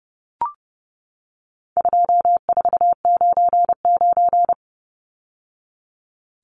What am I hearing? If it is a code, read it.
Morse code, heard as a keyed tone, about 30 wpm, 700 Hz. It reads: 2499
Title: Numbers - 4-Digit 30wpm